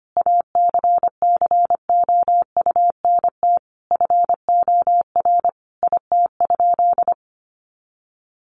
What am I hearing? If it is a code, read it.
Morse code, heard as a keyed tone, about 25 wpm, 700 Hz. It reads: ACCOUNT FOR IT?
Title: Sentences from Top 900 Words 25wpm